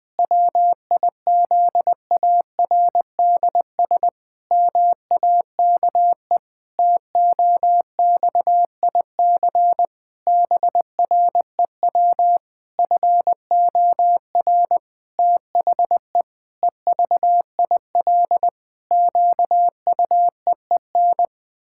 Morse code, heard as a keyed tone, about 21 wpm, 700 Hz. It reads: WIZARDS MAKE TOXIC BREW FOR THE EVIL QUEEN
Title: Pangram Challenge v2 - 20wpm